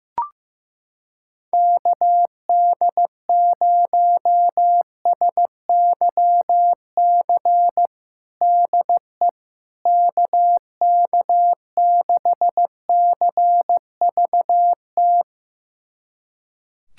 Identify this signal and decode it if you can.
Morse code, heard as a keyed tone, about 15 wpm, 700 Hz. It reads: KD0SYC DE KK6CVT
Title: USA Callsign DE Callsign 15wpm